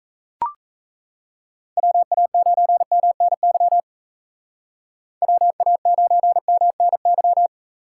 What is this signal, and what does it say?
Morse code, heard as a keyed tone, about 34 wpm, 700 Hz. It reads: WA9MNY WA9MNY
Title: Speed Racing - USA 2x3 Call Signs 28wpm